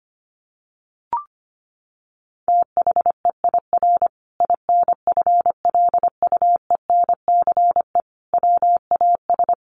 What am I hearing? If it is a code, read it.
Morse code, heard as a keyed tone, about 25 wpm, 700 Hz. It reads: THEIR INFLUENCE WAS
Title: Sentences from Top 1400 Words 25wpm